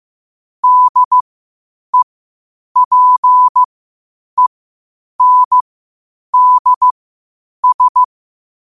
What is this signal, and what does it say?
Morse code, heard as a keyed tone, about 15 wpm, spaced as for 8 wpm, 1000 Hz. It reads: DEPENDS
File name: qotdfiles / 8wpm